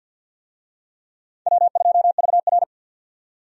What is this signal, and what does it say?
Morse code, heard as a keyed tone, about 52 wpm, 700 Hz. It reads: W2VR
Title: USA Call Signs - 4 Characters - 50wpm